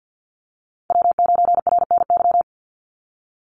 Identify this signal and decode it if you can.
Morse code, heard as a keyed tone, about 50 wpm, 700 Hz. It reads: W9RNY
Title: Call Signs - 50wpm